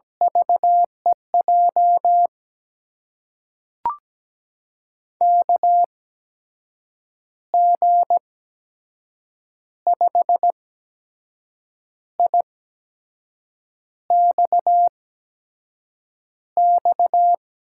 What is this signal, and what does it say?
Morse code, heard as a keyed tone, about 17 wpm, spaced as for 6 wpm, 700 Hz. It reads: VEJ KG5IXX
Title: USA Call Signs - 6 Characters - 8x Character Spacing - 17wpm